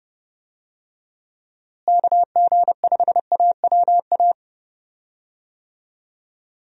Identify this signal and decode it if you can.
Morse code, heard as a keyed tone, about 30 wpm, 700 Hz. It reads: KG5AWA
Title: USA Call Signs - 6 Characters - Repeated 2x - 30wpm